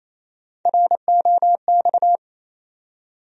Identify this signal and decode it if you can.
Morse code, heard as a keyed tone, about 28 wpm, 700 Hz. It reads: ROX
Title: USA Call Sign Postfixes - 3 Letters 28wpm